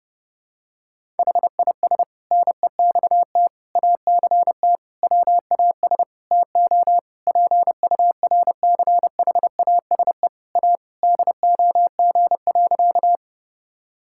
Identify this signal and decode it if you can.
Morse code, heard as a keyed tone, about 30 wpm, 700 Hz. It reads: HIS NEXT ACT WAS TO PURCHASE A DOG.
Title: Sentences from Top 1000 Words - Review Part 5 - 30wpm